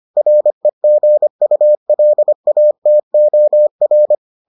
Morse code, heard as a keyed tone, about 25 wpm, 600 Hz. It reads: REGULATOR